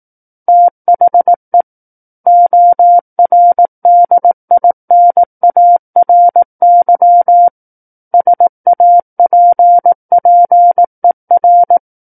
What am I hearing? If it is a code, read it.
Morse code, heard as a keyed tone, about 18 wpm, 700 Hz. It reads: THE ORDINARY SAPPER
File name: Book One / 09-Chapter 9 - 18 WPM